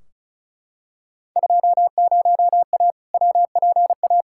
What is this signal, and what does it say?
Morse code, heard as a keyed tone, about 36 wpm, 700 Hz. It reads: 20A WPA
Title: Field Day Exchanges Encore - 35wpm